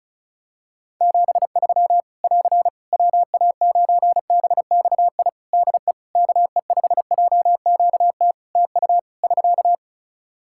Morse code, heard as a keyed tone, about 36 wpm, 700 Hz. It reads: 73 <AR> WA9BXI DE KE5JQT TU <SK>
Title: Practice QSO - Exchange 4 - Ending - 35wpm